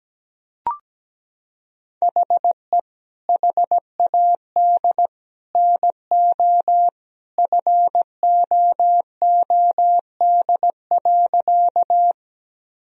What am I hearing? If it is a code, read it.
Morse code, heard as a keyed tone, about 17 wpm, 700 Hz. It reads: HE HAD NO FOOD.
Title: Sentences from Top 1000 Words - Review Part 1 - 17wpm